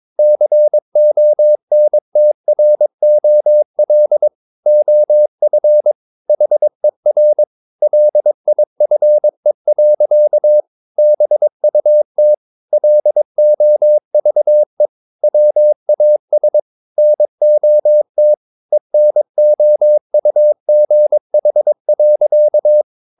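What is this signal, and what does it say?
Morse code, heard as a keyed tone, about 22 wpm, 600 Hz. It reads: CONTROL OF HER LIFE. BUT LOVE WAS NOT ENOUGH.
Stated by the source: Morse code practice at 22 wpm, sending text from an English novel.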